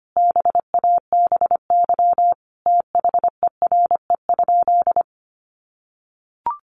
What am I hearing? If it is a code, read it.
Morse code, heard as a keyed tone, about 25 wpm, 700 Hz. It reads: BABY THERE?
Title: Sentences from Top 1300 Words 25wpm